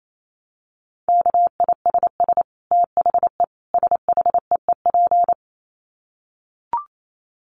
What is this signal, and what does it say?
Morse code, heard as a keyed tone, about 28 wpm, 700 Hz. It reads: KISS THE SHEEP
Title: Dit Challenge 28wpm